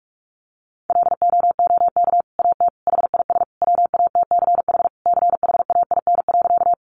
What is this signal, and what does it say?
Morse code, heard as a keyed tone, about 44 wpm, 700 Hz. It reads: LOOK AT HIS WATCH CHAIN.
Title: Sentences from Top 1500 Words 45wpm